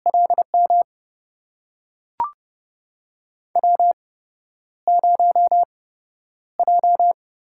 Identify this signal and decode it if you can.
Morse code, heard as a keyed tone, about 29 wpm, spaced as for 11 wpm, 700 Hz. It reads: LM W0J
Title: USA Call Signs - 5 Characters - 8x Character Spacing - 30wpm